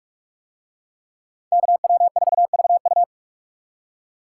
Morse code, heard as a keyed tone, about 44 wpm, 700 Hz. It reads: KW4VU